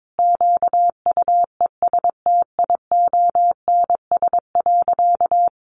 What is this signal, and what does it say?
Morse code, heard as a keyed tone, about 22 wpm, 700 Hz. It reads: QUESTIONS.
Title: Additional Easy Phrases 22wpm